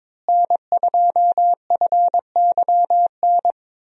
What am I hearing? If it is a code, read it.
Morse code, heard as a keyed tone, about 22 wpm, 700 Hz. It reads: N2FYN
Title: USA Call Signs - 5 Characters - Repeated 2x - 22wpm